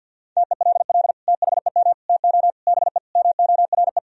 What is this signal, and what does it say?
Morse code, heard as a keyed tone, about 50 wpm, 700 Hz. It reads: TELL THEM TO BE MORE
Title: Sentences from Top 1000 Words - Review Part 3 - 50wpm